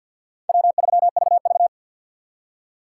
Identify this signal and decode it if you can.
Morse code, heard as a keyed tone, about 61 wpm, 700 Hz. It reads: W3VV
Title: Speed Racing - USA 1x2 Call Signs 50wpm